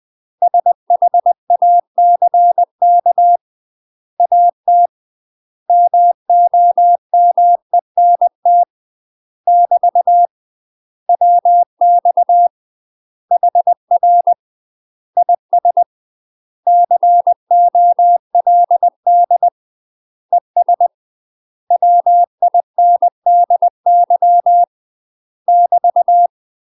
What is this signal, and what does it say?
Morse code, heard as a keyed tone, about 20 wpm, 700 Hz. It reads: SHACK AT MOMENT <BT> WX HR IS COLD ES WINDY <BT>
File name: UK MASTER QSOs TRAINING / 20WPM / UK MASTER QSOs 0071